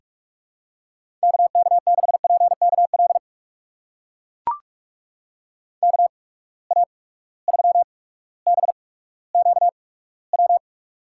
Morse code, heard as a keyed tone, about 44 wpm, spaced as for 19 wpm, 700 Hz. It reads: KK6PKL KA3BQW
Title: USA Call Signs - 6 Characters - 8x Character Spacing - 45wpm